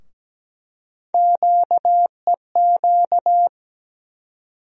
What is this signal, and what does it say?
Morse code, heard as a keyed tone, about 17 wpm, 700 Hz. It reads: QEQ
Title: USA Call Sign Postfixes - 3 Letters 17wpm